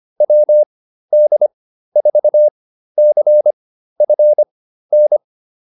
Morse code, heard as a keyed tone, about 25 wpm, spaced as for 13 wpm, 600 Hz. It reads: WD4CFN
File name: QSO 207 13